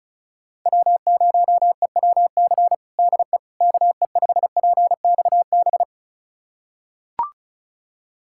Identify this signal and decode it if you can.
Morse code, heard as a keyed tone, about 36 wpm, 700 Hz. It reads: W0EWC DE KE5PXB
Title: USA Callsign DE Callsign 35wpm